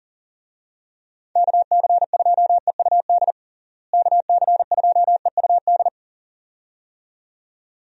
Morse code, heard as a keyed tone, about 40 wpm, 700 Hz. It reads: KC2EUD KC2EUD
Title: USA Call Signs - 6 Characters - Repeated 2x - 40wpm